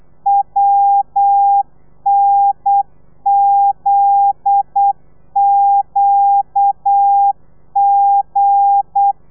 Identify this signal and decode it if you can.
Morse code, heard as a keyed tone, about 8 wpm, 800 Hz. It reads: WNZQG
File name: telegrafia / i2tfj / Lez9 35